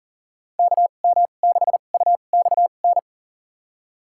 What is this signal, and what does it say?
Morse code, heard as a keyed tone, about 40 wpm, spaced as for 29 wpm, 700 Hz. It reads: KM6UXN
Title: USA Call Signs - 6 Characters - 2x Character Spacing - 40wpm